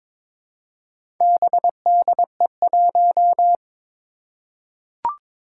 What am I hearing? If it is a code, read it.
Morse code, heard as a keyed tone, about 21 wpm, 700 Hz. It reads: BDE1
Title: Four Challenging Letter-Number Combinations 22wpm